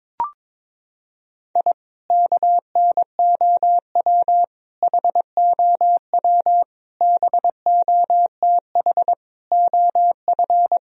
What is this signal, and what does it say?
Morse code, heard as a keyed tone, about 22 wpm, 700 Hz. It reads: I KNOW HOW BOTH OF
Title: Sentences from Top 1000 Words - Review Part 1 - 22wpm